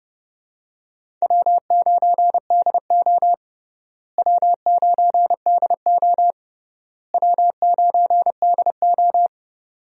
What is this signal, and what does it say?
Morse code, heard as a keyed tone, about 31 wpm, 700 Hz. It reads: W9DO W9DO W9DO
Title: USA Call Signs - 4 Characters - Repeated 3x - 30wpm